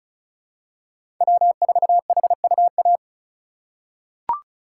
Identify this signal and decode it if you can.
Morse code, heard as a keyed tone, about 36 wpm, 690 Hz. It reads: W4HUA
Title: USA Call Signs - 5 Characters - 3x Character Spacing - 35wpm